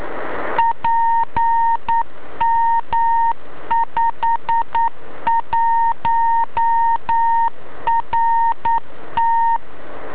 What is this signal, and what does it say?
Morse code, heard as a keyed tone, about 9 wpm, 920 Hz. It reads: PM51RT
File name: beacon / ja6ybr